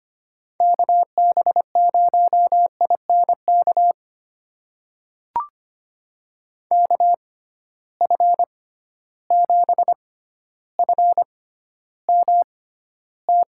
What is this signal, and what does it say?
Morse code, heard as a keyed tone, about 25 wpm, spaced as for 13 wpm, 700 Hz. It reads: KB0INK KF7FMT
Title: USA Call Signs - 6 Characters - 6x Character Spacing - 25wpm